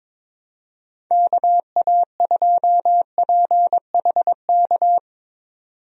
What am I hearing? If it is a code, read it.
Morse code, heard as a keyed tone, about 22 wpm, 700 Hz. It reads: KA2PHK
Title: USA Call Signs - 6 Characters - 22wpm